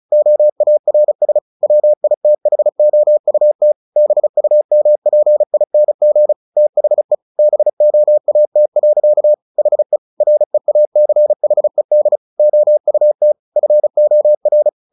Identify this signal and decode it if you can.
Morse code, heard as a keyed tone, about 34 wpm, 600 Hz. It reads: OARS WITHOUT BUMPING THE BOAT. HE REACHED OUT FOR
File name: the old man and the sea / the old man and the sea 1 35wpm